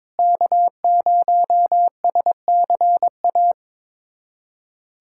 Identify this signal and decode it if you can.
Morse code, heard as a keyed tone, about 22 wpm, 700 Hz. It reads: K0SCA
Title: USA Callsign DE Callsign - 5 Characters - 22wpm